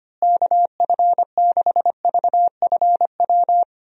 Morse code, heard as a keyed tone, about 25 wpm, 700 Hz. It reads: KF6VFW
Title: USA Call Signs - 6 Characters - Repeated 3x - 25wpm